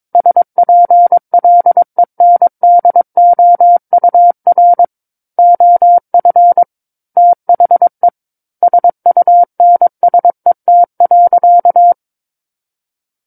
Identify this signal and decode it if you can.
Morse code, heard as a keyed tone, about 22 wpm, 700 Hz. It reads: SPLENDOUR OF THE SUNSET.
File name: Book One / 13-Chapter 13 - 22 WPM